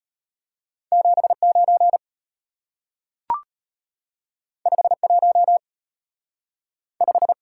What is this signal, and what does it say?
Morse code, heard as a keyed tone, about 34 wpm, 700 Hz. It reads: 79 51 5